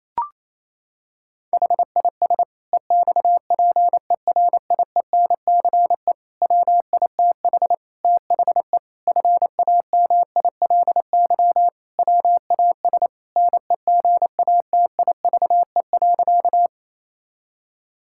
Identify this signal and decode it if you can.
Morse code, heard as a keyed tone, about 28 wpm, 700 Hz. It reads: HIS EXPERIENCE WITH THE FAMILY WAS NEGATIVE.
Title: Sentences from Top 1000 Words - Review Part 6 - 28wpm